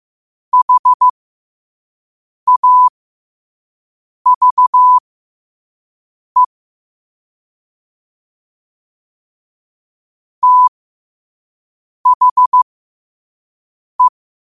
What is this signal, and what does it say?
Morse code, heard as a keyed tone, about 15 wpm, spaced as for 5 wpm, 1000 Hz. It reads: HAVE THE
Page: Quotes for Wed, 13 Aug 2025 in Morse Code at 5 words per minute.